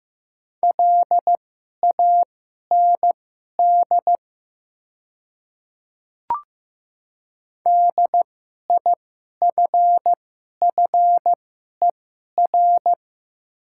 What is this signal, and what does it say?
Morse code, heard as a keyed tone, about 15 wpm, spaced as for 11 wpm, 700 Hz. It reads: LAND DIFFER
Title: 2x Character Spacing - Top 500 Words - 15wpm